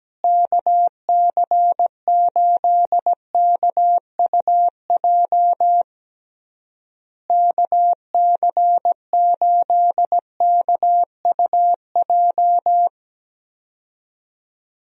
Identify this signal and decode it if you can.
Morse code, heard as a keyed tone, about 17 wpm, 700 Hz. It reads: KC8KUJ KC8KUJ
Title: USA Call Signs - 6 Characters - Repeated 3x - 17wpm